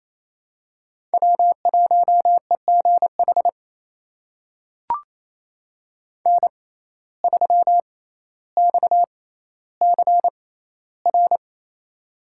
Morse code, heard as a keyed tone, about 27 wpm, spaced as for 14 wpm, 700 Hz. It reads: W1EGH N3XCR
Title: USA Call Signs - 5 Characters - 6x Character Spacing - 28wpm